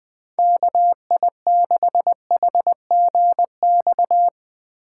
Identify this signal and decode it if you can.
Morse code, heard as a keyed tone, about 20 wpm, 700 Hz. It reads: KI6HGX